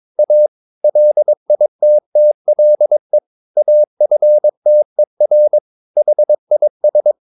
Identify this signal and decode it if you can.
Morse code, heard as a keyed tone, about 22 wpm, 600 Hz. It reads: A LITTLE AFTER HIS
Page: Romanen Never af Ken Follet på engelsk.